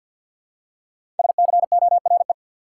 Morse code, heard as a keyed tone, about 52 wpm, 700 Hz. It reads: SCORE
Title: Five Letter Words - 50wpm